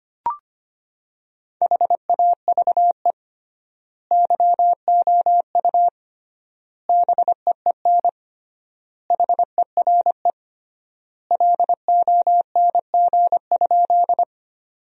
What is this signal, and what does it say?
Morse code, heard as a keyed tone, about 25 wpm, 700 Hz. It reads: HAVE YOU BEEN HERE LONG?
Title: 3x Word-Spacing - Sentences from Top 200 Words - 25wpm